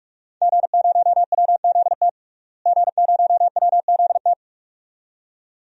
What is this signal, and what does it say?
Morse code, heard as a keyed tone, about 43 wpm, 700 Hz. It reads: G0WZT G0WZT
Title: DXCC-100 Call Signs - 5 Characters - Repeated 2x - 45wpm